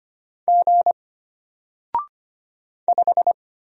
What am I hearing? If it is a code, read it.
Morse code, heard as a keyed tone, about 25 wpm, 700 Hz. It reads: G 5
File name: Single-Letter-Number-25wpm